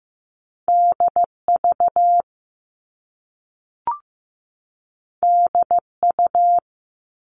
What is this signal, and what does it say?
Morse code, heard as a keyed tone, about 15 wpm, 700 Hz. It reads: DV DU
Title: All Cut Numbers - 2-Digit 15wpm